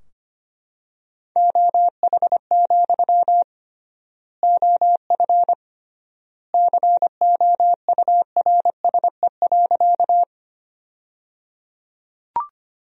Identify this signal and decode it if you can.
Morse code, heard as a keyed tone, about 25 wpm, 700 Hz. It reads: OH, OF COURSE.
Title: 3x Word-Spacing - Sentences from Top 500 Words - 25wpm